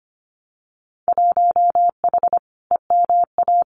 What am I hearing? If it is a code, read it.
Morse code, heard as a keyed tone, about 25 wpm, 700 Hz. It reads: 1H EMA
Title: Winter Field Day Exchanges 25wpm